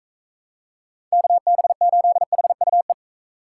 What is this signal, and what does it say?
Morse code, heard as a keyed tone, about 42 wpm, 700 Hz. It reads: KB8HUE